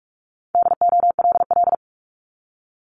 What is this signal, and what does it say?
Morse code, heard as a keyed tone, about 46 wpm, 700 Hz. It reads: DOLL
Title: Multi-letter Phonemes Repeated 3x 45wpm